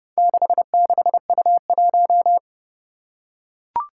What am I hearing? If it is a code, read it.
Morse code, heard as a keyed tone, about 32 wpm, 700 Hz. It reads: 66U1
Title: Four Challenging Letter-Number Combinations 30wpm